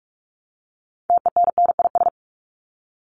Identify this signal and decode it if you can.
Morse code, heard as a keyed tone, about 44 wpm, 700 Hz. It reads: TENNIS
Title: Top 1501-1750 Words 45wpm